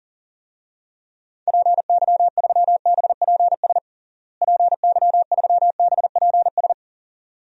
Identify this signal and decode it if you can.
Morse code, heard as a keyed tone, about 37 wpm, 700 Hz. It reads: PY3BPS PY3BPS
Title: DXCC-100 Call Signs - 6 Characters - Repeated 2x - 40wpm